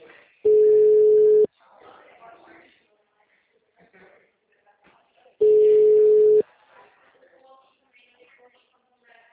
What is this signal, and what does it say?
Ringback tone (Europe, 425 Hz), with speech.
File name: out- -2006- - - .30375